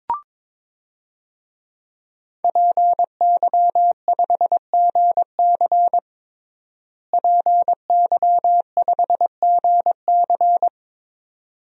Morse code, heard as a keyed tone, about 22 wpm, 700 Hz. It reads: PY5GC PY5GC
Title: DXCC-100 Call Signs - 5 Characters - Repeated 2x - 22wpm